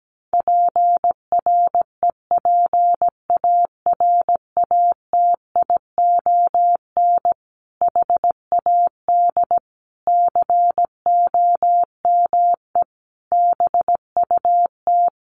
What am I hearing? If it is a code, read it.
Morse code, heard as a keyed tone, about 17 wpm, 700 Hz. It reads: PREPARATION HAD COME BUT
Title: Sentences from Top 1400 Words 17wpm